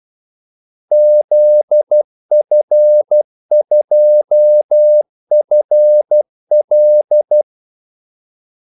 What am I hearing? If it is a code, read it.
Morse code, heard as a keyed tone, about 12 wpm, 600 Hz. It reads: ZF2FL
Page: 2 ekstra ordmellemrum mellem kaldesignalerne